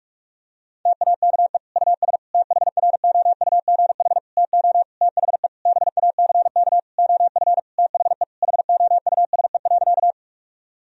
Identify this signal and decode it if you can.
Morse code, heard as a keyed tone, about 41 wpm, 700 Hz. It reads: TAKE US THROUGH TO THE BACK OF THE HOUSE.
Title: Sentences from Top 1000 Words - Review Part 1 - 45wpm